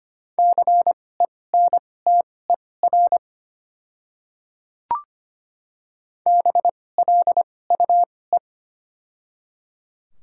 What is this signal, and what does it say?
Morse code, heard as a keyed tone, about 25 wpm, spaced as for 18 wpm, 700 Hz. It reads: CENTER BLUE
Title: 2x Character Spacing - Top 500 Words - 25wpm